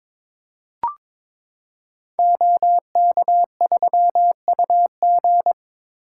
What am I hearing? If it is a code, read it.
Morse code, heard as a keyed tone, about 23 wpm, 700 Hz. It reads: OK3UG
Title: Code Groups - 5 Letter:Numbers - 22wpm